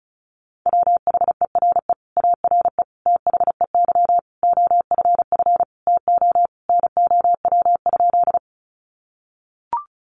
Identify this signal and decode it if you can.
Morse code, heard as a keyed tone, about 34 wpm, 700 Hz. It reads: WHERE ARE THEY OFF TO NOW?
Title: Sentences from Top 200 Words 35wpm